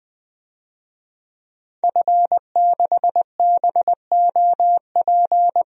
Morse code, heard as a keyed tone, about 20 wpm, 700 Hz. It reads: F6BOP
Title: DXCC-100 Call Signs - 5 Characters - 20wpm